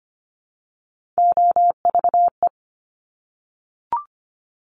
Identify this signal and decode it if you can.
Morse code, heard as a keyed tone, about 25 wpm, 700 Hz. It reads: OVE
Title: Top Three Letter Combinations in Words 25wpm